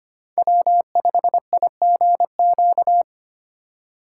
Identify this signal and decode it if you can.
Morse code, heard as a keyed tone, about 25 wpm, 700 Hz. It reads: W5IGQ
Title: USA Call Signs - 2x Character Spacing - 25wpm